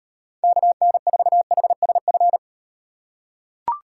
Morse code, heard as a keyed tone, about 38 wpm, 700 Hz. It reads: KN4HSF